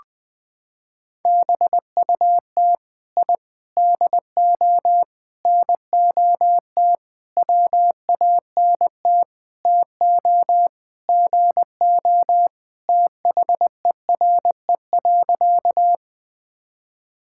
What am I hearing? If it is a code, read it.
Morse code, heard as a keyed tone, about 21 wpm, 700 Hz. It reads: BUT I DO NOT WANT TO GO THERE.
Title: Sentences with 5-Word Phrases - 20wpm